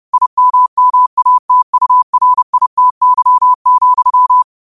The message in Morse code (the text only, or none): IMMATURITY,